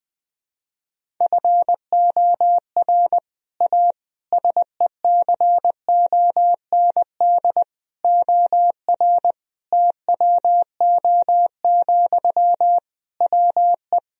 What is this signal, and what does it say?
Morse code, heard as a keyed tone, about 20 wpm, 700 Hz. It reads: FOR A SECOND OR TWO, WE
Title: Sentences from Top 1000 Words - Review Part 1 - 20wpm